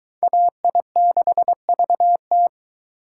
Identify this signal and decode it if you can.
Morse code, heard as a keyed tone, about 22 wpm, 700 Hz. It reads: AI6VT